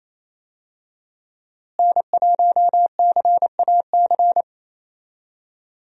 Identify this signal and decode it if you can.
Morse code, heard as a keyed tone, about 28 wpm, 700 Hz. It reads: N1CAC
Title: USA Call Signs - 5 Characters - Repeated 2x - 28wpm